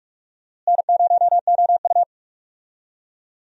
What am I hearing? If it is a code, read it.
Morse code, heard as a keyed tone, about 46 wpm, 700 Hz. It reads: N0OU
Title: USA Call Signs - 4 Characters - Repeated 3x - 45wpm